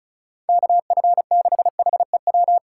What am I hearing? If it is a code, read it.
Morse code, heard as a keyed tone, about 36 wpm, 700 Hz. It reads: KF6HEW
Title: USA Call Signs - 6 Characters - 4x Character Spacing - 35wpm